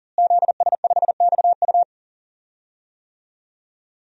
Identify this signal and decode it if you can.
Morse code, heard as a keyed tone, about 40 wpm, 700 Hz. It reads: ZS5XU
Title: DXCC-100 Call Signs - 5 Characters - Repeated 3x - 40wpm